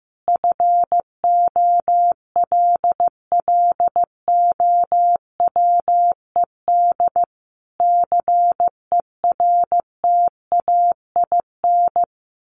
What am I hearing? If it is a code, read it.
Morse code, heard as a keyed tone, about 15 wpm, 700 Hz. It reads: FOLLOWED CERTAIN
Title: A Sound Of Thunder 15wpm